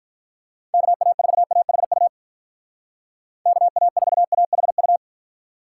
Morse code, heard as a keyed tone, about 43 wpm, 700 Hz. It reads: KA4AHU KA4AHU
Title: Speed Racing - USA 2x3 Call Signs 35wpm